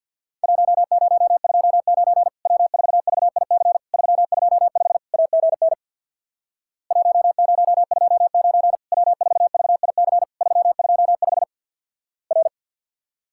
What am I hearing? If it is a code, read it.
Morse code, heard as a keyed tone, about 49 wpm, 700 Hz. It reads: 1029 W4VIC 325 AGN 1029 W4VIC 325 R
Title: Worked All Europe DX Contest - QTC Exchange with Repeats - 50wpm